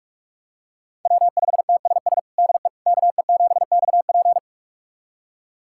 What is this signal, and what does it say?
Morse code, heard as a keyed tone, about 46 wpm, 700 Hz. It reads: W5TSS DE KE7XP
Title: USA Callsign DE Callsign 45wpm